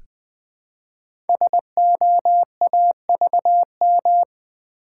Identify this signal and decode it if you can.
Morse code, heard as a keyed tone, about 20 wpm, 700 Hz. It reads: SOAVM
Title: Code Groups - 5 Letters - 20wpm